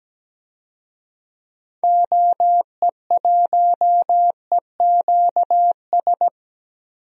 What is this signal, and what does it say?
Morse code, heard as a keyed tone, about 17 wpm, 700 Hz. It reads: OE1EQS
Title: DXCC-100 Call Signs - 6 Characters - Repeated 3x - 17wpm